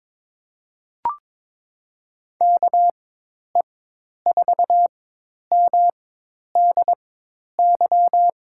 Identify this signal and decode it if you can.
Morse code, heard as a keyed tone, about 22 wpm, spaced as for 10 wpm, 700 Hz. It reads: KE4MDY
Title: USA Call Signs - 4x Character Spacing - 22wpm